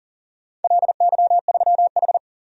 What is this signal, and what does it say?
Morse code, heard as a keyed tone, about 40 wpm, 700 Hz. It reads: LY3H
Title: DXCC-100 Call Signs - 4 Characters - 40wpm